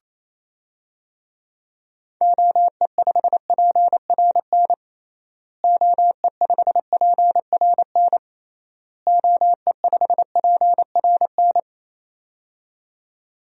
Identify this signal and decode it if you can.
Morse code, heard as a keyed tone, about 28 wpm, 700 Hz. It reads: OE5PRN OE5PRN OE5PRN
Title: DXCC-100 Call Signs - 6 Characters - Repeated 3x - 28wpm